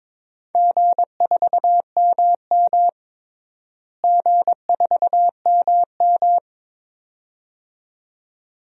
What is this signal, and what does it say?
Morse code, heard as a keyed tone, about 22 wpm, 700 Hz. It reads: G4MM G4MM
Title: DXCC-100 Call Signs - 4 Characters - Repeated 2x - 22wpm